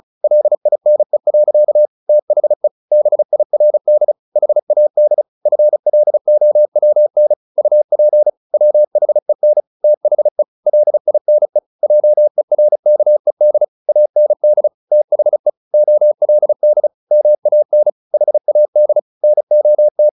Morse code, heard as a keyed tone, about 34 wpm, 600 Hz. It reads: LINE. THE BIRD HAD FLOWN UP WHEN THE LINE JERKED AND THE OLD MAN HAD NOT
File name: the old man and the sea / the old man and the sea 1 35wpm